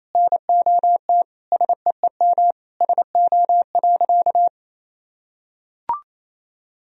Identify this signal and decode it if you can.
Morse code, heard as a keyed tone, about 28 wpm, 700 Hz. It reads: NOT SEEM SO.
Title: Sentences from Top 1000 Words - Review Part 1 - 28wpm